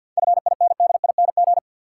Morse code, heard as a keyed tone, about 48 wpm, 700 Hz. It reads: FINDING
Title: Sentences from Top 1100 Words - New Word Drill - 50wpm